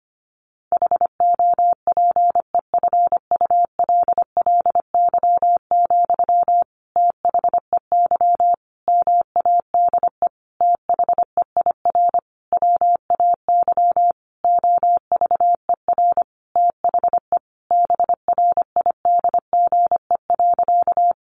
Morse code, heard as a keyed tone, about 25 wpm, 700 Hz. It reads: HOPEFULLY, THEY MADE THEIR WAY OVER THE BRIDGE.
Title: Sentences from Top 1500 Words 25wpm